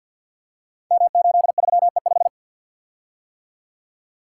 Morse code, heard as a keyed tone, about 47 wpm, 700 Hz. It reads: M83E5
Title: Code Groups - 5 Letter:Numbers - 50wpm